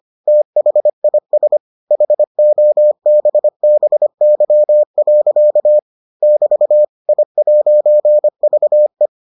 Morse code, heard as a keyed tone, about 25 wpm, 600 Hz. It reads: THIS HOBBY. <BT> I'VE